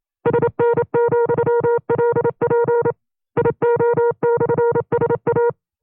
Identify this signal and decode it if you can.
Morse code, heard as a keyed tone, about 27 wpm, 470 Hz. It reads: SN,LP IO/SA
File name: FreeFiles / CW-practice / 25WPM-side1